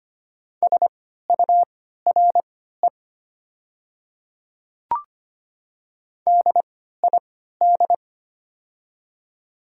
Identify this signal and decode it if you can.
Morse code, heard as a keyed tone, about 25 wpm, spaced as for 14 wpm, 700 Hz. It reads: SURE DID
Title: 3x Character Spacing - Top 300 Words - 25wpm